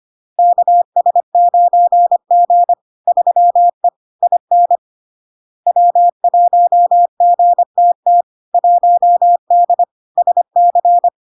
Morse code, heard as a keyed tone, about 25 wpm, 700 Hz. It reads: KS9G 3E IN W1GTT 1D SC
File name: LARK-FD-2022-Log-in-CW-25wpm-700-Hz